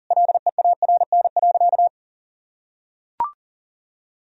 Morse code, heard as a keyed tone, about 40 wpm, 700 Hz. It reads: LEARN.
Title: Sentences from Top 1000 Words - Review Part 3 - 40wpm